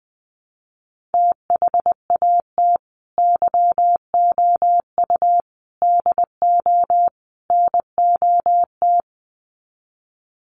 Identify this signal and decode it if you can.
Morse code, heard as a keyed tone, about 20 wpm, 700 Hz. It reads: THAT YOU DO NOT
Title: Sets of 4 Words 20wpm